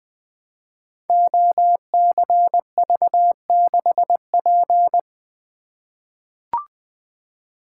Morse code, heard as a keyed tone, about 20 wpm, 700 Hz. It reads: OCV6P
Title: Code Groups - 5 Letter:Numbers - 20wpm